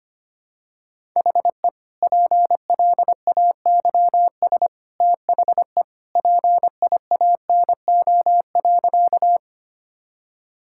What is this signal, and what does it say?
Morse code, heard as a keyed tone, about 25 wpm, 700 Hz. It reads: HE PLAYS THE PIANO.
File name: Easy Sentences with Hints - 25wpm